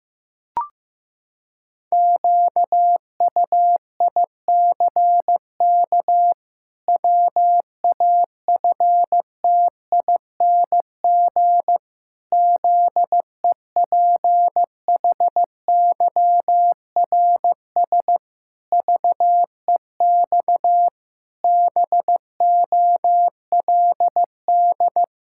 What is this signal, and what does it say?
Morse code, heard as a keyed tone, about 16 wpm, 700 Hz. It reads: QUICK WAFTING ZEPHYRS VEX BOLD
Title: Pangram Challenge v2 - 15wpm